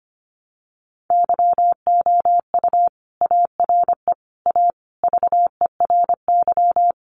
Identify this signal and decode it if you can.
Morse code, heard as a keyed tone, about 25 wpm, 700 Hz. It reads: YOU ARE A VERY
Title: Sentences from Top 1000 Words 25wpm